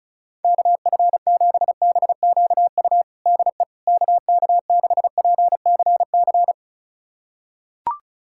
Morse code, heard as a keyed tone, about 36 wpm, 700 Hz. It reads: KF7BQU DE KK6PCC
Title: USA Callsign DE Callsign 35wpm